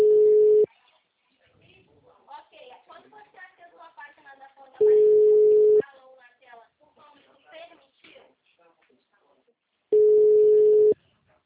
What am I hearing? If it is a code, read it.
Ringback tone (Europe, 425 Hz), with speech.